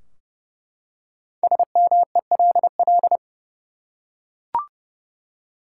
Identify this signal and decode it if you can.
Morse code, heard as a keyed tone, about 30 wpm, 700 Hz. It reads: SMELL
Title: Five Letter Words - 30wpm